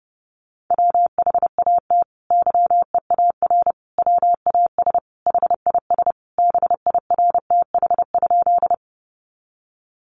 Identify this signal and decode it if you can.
Morse code, heard as a keyed tone, about 30 wpm, 700 Hz. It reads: WHAT YEAR WAS HIS BIRTH?
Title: Sentences from Top 800 Words 30wpm